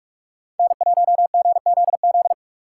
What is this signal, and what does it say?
Morse code, heard as a keyed tone, about 44 wpm, 700 Hz. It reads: N1GZZ
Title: USA Call Signs - 5 Characters - 4x Character Spacing - 45wpm